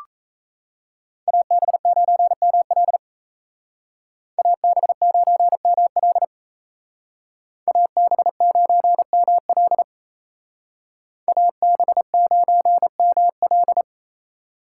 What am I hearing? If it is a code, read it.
Morse code, heard as a keyed tone, about 37 wpm, 700 Hz. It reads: AB9ML AB9ML AB9ML AB9ML